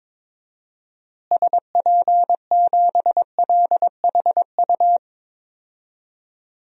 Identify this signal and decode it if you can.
Morse code, heard as a keyed tone, about 22 wpm, 700 Hz. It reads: SP7LHU
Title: DXCC-100 Call Signs - 6 Characters - Repeated 3x - 22wpm